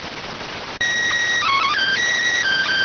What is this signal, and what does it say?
SSTV (slow-scan television), PD 290.